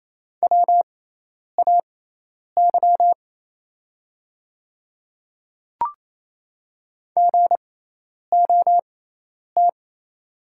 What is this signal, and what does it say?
Morse code, heard as a keyed tone, about 28 wpm, spaced as for 10 wpm, 700 Hz. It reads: WAY GOT
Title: 6x Character Spacing - Top 300 Words - 28wpm